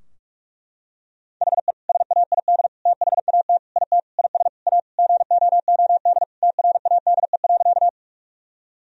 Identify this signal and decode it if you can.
Morse code, heard as a keyed tone, about 42 wpm, 700 Hz. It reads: HE SAID THAT IT IS A GOOD TRADE.
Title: Sentences from Top 1000 Words - Review Part 4 - 45wpm